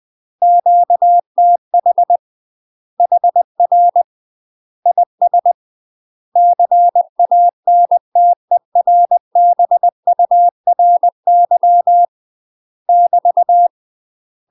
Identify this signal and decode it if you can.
Morse code, heard as a keyed tone, about 20 wpm, 700 Hz. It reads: QTH HR IS CANTERBURY <BT>
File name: UK MASTER QSOs TRAINING / 20WPM / UK MASTER QSOs 0015